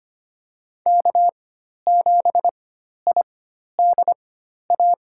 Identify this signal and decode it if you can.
Morse code, heard as a keyed tone, about 25 wpm, spaced as for 12 wpm, 700 Hz. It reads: K7IDA
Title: USA Call Signs - 5 Characters - 4x Character Spacing - 25wpm